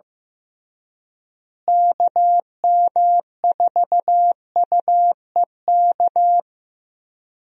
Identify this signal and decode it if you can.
Morse code, heard as a keyed tone, about 15 wpm, 700 Hz. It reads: KM4UEK
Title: USA Call Signs - 6 Characters - Repeated 3x - 15wpm